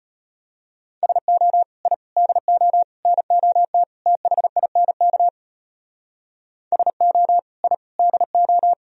Morse code, heard as a keyed tone, about 32 wpm, 700 Hz. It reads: SO I DO NOT THINK SO I DO